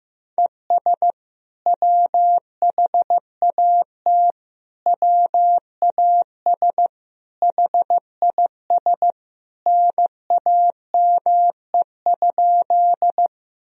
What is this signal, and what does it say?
Morse code, heard as a keyed tone, about 15 wpm, 700 Hz. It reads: ES WHAT WAS HIS NAME?
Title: Sentences from Top 200 Words and Abbreviations 15wpm